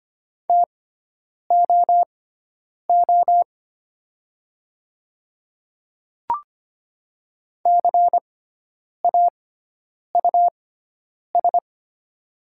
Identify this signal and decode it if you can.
Morse code, heard as a keyed tone, about 25 wpm, spaced as for 9 wpm, 700 Hz. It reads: TOO CAUS
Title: 6x Character Spacing - Top 500 Words - 25wpm